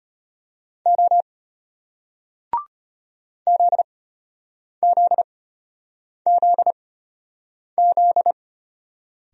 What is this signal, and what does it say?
Morse code, heard as a keyed tone, about 34 wpm, spaced as for 8 wpm, 700 Hz. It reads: O ZZZZ